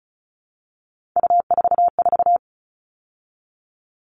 Morse code, heard as a keyed tone, about 34 wpm, 700 Hz. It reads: U44
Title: All Cut Numbers - 3-Digit 35wpm